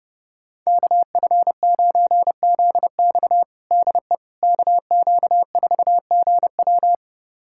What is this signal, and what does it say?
Morse code, heard as a keyed tone, about 30 wpm, 700 Hz. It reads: KF9ZX DE KQ4GW
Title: USA Callsign DE Callsign - 5 Characters - 30wpm